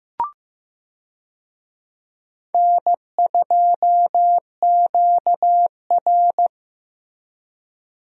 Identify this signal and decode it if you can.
Morse code, heard as a keyed tone, about 15 wpm, 700 Hz. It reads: N2QR
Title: USA Call Signs - 4 Characters - Repeated 3x - 15wpm